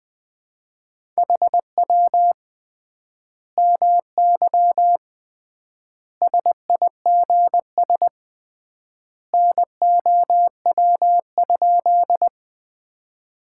Morse code, heard as a keyed tone, about 20 wpm, 700 Hz. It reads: HW MY SIGS NOW?
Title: Common QSO Phrases - 3x Word-Spacing - 20wpm